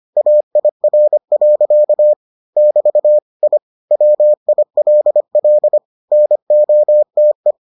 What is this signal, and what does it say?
Morse code, heard as a keyed tone, about 25 wpm, 600 Hz. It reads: AIR. <BT> I WILL NOTE